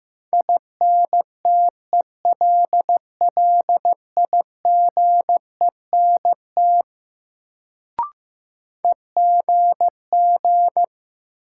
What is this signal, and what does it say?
Morse code, heard as a keyed tone, about 16 wpm, 700 Hz. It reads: INTELLIGENT EGG
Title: Sentences from Top 1000 Words - Review Part 7 - New Word Drill - 15wpm